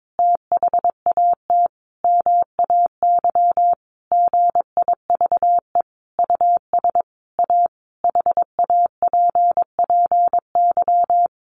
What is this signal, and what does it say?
Morse code, heard as a keyed tone, about 22 wpm, 700 Hz. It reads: THAT MAY GIVE US A HAPPY
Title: Sentences from Top 1100 Words 22wpm